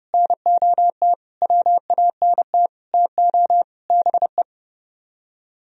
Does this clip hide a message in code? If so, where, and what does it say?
Yes: Morse code from 0.1 to 4.4 s, heard as a keyed tone, about 38 wpm, 700 Hz. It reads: NOT WANT TO BE